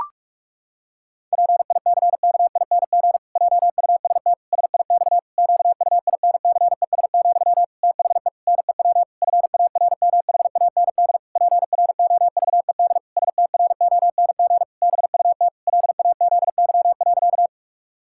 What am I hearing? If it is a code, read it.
Morse code, heard as a keyed tone, about 48 wpm, 700 Hz. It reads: PICKING JUST SIX QUINCES, THE NEW FARMHAND PROVED STRONG BUT LAZY.
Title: Pangram Challenge v2 - 45wpm